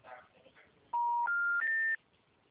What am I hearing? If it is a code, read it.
Special information tone.
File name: out- -2009- - - .38